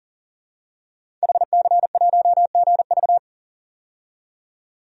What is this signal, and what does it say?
Morse code, heard as a keyed tone, about 40 wpm, 700 Hz. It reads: HC1GV
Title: DXCC-100 Call Signs - 5 Characters - 40wpm